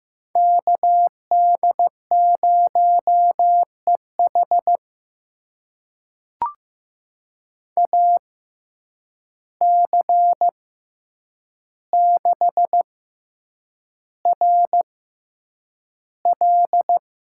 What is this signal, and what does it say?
Morse code, heard as a keyed tone, about 15 wpm, spaced as for 8 wpm, 700 Hz. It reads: KD0EH AC6RL
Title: USA Call Signs - 5 Characters - 6x Character Spacing - 15wpm